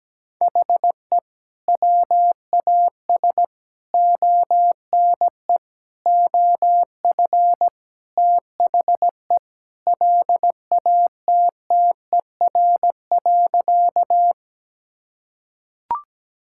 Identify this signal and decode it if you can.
Morse code, heard as a keyed tone, about 17 wpm, 700 Hz. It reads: HE WAS ONE OF THE LATTER.
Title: Sentences from Top 1600 Words 17wpm